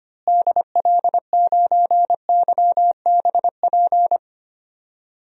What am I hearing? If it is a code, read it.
Morse code, heard as a keyed tone, about 25 wpm, 700 Hz. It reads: DL9YBP
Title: DXCC-100 Call Signs - 6 Characters - Repeated 2x - 25wpm